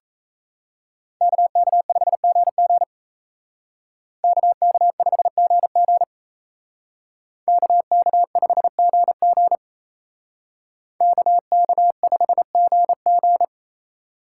Morse code, heard as a keyed tone, about 36 wpm, 700 Hz. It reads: KK5GG KK5GG KK5GG KK5GG